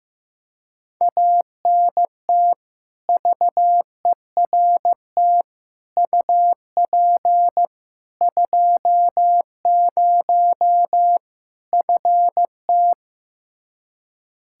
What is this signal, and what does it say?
Morse code, heard as a keyed tone, about 15 wpm, 700 Hz. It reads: ANT VERT UP 20 FT
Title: Practice QSO - Exchange 2 - Info - 15wpm